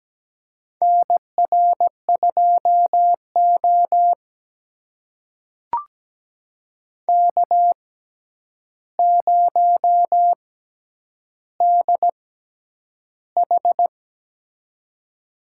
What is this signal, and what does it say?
Morse code, heard as a keyed tone, about 17 wpm, spaced as for 9 wpm, 700 Hz. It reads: NR2O K0DH
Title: USA Call Signs - 4 Characters - 6x Character Spacing - 17wpm